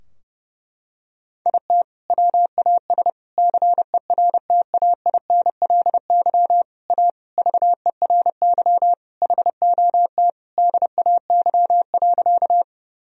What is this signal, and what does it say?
Morse code, heard as a keyed tone, about 30 wpm, 700 Hz. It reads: IT WAS CERTAINLY A VERY HOT DAY.
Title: Sentences from Top 1000 Words - Review Part 3 - 30wpm